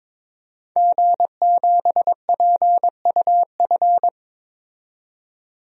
Morse code, heard as a keyed tone, about 22 wpm, 700 Hz. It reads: G7PUF
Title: DXCC-100 Call Signs - 5 Characters - 22wpm